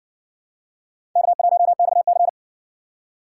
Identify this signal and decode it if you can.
Morse code, heard as a keyed tone, about 60 wpm, 700 Hz. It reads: K1XC